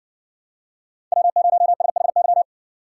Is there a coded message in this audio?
Yes: Morse code from 1.1 to 2.4 s, heard as a keyed tone, about 60 wpm, 700 Hz. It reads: W0SHY